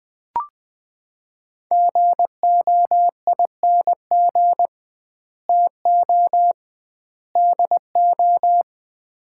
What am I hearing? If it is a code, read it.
Morse code, heard as a keyed tone, about 20 wpm, 700 Hz. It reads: GOING TO DO
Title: 2x Word-Spacing - Sets of 3 Words - 20wpm